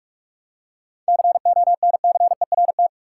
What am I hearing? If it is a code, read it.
Morse code, heard as a keyed tone, about 41 wpm, 700 Hz. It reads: CONCERT
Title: Seven Letter Words - 45wpm